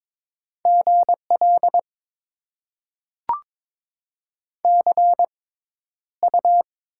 Morse code, heard as a keyed tone, about 21 wpm, spaced as for 11 wpm, 700 Hz. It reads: GL CU
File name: Top 90 QSO Elements - 6x Character Spacing - 22wpm